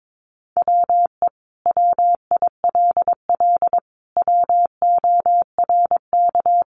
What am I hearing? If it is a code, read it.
Morse code, heard as a keyed tone, about 22 wpm, 700 Hz. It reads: WE WILL WORK
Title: Sentences from Top 1100 Words 22wpm